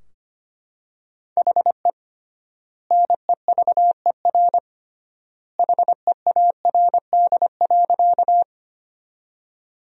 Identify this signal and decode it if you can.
Morse code, heard as a keyed tone, about 25 wpm, 700 Hz. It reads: HE NEVER HEARD.
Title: 3x Word-Spacing - Sentences from Top 400 Words - 25wpm